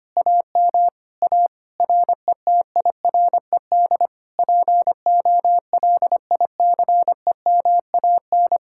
Morse code, heard as a keyed tone, about 24 wpm, 700 Hz. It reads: AM A RETIRED POLICEMAN
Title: Speed-Racing Common QSO Phrases Encore - 25wpm